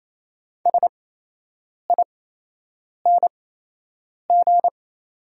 Morse code, heard as a keyed tone, about 28 wpm, spaced as for 8 wpm, 700 Hz. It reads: SING
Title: 8x Character Spacing - Top 500 Words - 28wpm